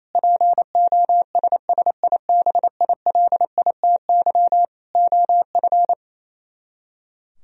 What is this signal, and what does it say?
Morse code, heard as a keyed tone, about 26 wpm, 700 Hz. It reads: POSSIBILITY OF
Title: Speed-Racing - Sets of 3 Words 28wpm